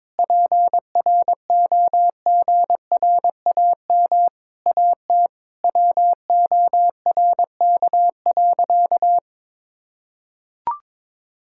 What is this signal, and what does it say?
Morse code, heard as a keyed tone, about 21 wpm, 700 Hz. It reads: PROGRAM AT WORK.
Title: Sentences from Top 1000 Words - Review Part 5 - 22wpm